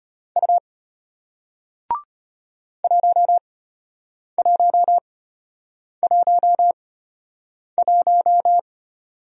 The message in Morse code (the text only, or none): U 1111